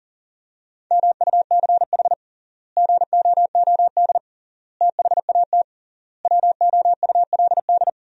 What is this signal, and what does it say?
Morse code, heard as a keyed tone, about 37 wpm, 700 Hz. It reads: MUCH GOOD THAT WOULD
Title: 3x Word-Spacing - Sentences from Top 200 Words - 40wpm